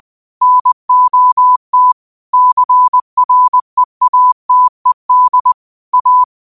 Morse code, heard as a keyed tone, about 20 wpm, 1000 Hz. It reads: NOT CREATED A
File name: qotdfiles / 20wpm